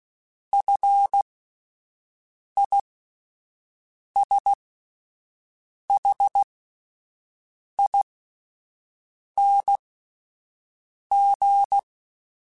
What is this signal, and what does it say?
Morse code, heard as a keyed tone, about 16 wpm, spaced as for 5 wpm, 800 Hz. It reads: FISHING
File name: Fs / Single F mp3s / Single F Prefixes 3x2 Twice @5